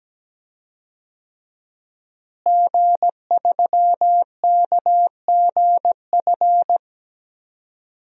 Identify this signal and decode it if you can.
Morse code, heard as a keyed tone, about 17 wpm, 700 Hz. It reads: G3KGF
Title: DXCC-100 Call Signs - 5 Characters - Repeated 2x - 17wpm